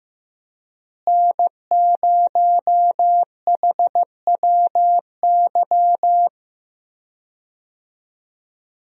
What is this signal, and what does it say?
Morse code, heard as a keyed tone, about 15 wpm, 700 Hz. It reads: N0HWY
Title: USA Call Signs - 5 Characters - Repeated 3x - 15wpm